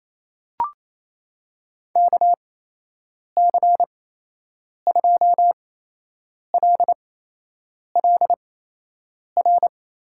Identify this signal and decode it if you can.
Morse code, heard as a keyed tone, about 28 wpm, spaced as for 8 wpm, 700 Hz. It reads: KC2LLR
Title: USA Call Signs - 8x Character Spacing - 28wpm